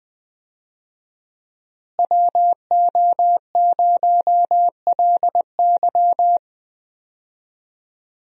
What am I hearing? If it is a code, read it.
Morse code, heard as a keyed tone, about 19 wpm, 700 Hz. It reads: WO0LY
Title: USA Call Signs - 5 Characters - Repeated 3x - 20wpm